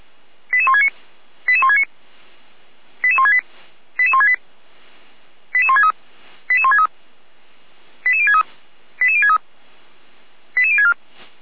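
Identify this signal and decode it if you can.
Five-tone selective call (ZVEI), heard as quick runs of five tones, tones of about 80 ms. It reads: 88169 88169 88168 88168 88163 88163 88063 88063 88064
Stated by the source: ZVEI five-tone selective call received by SDR.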